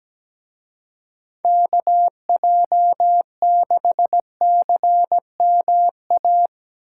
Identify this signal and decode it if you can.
Morse code, heard as a keyed tone, about 19 wpm, 700 Hz. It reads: KJ6CMA
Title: Speed Racing - USA 2x3 Call Signs 17wpm